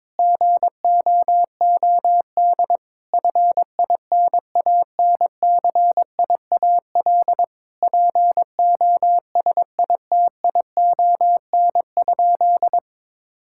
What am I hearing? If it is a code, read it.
Morse code, heard as a keyed tone, about 22 wpm, 700 Hz. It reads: GOOD FINANCIAL POSITION?
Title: Sentences from Top 1000 Words - Review Part 5 - 22wpm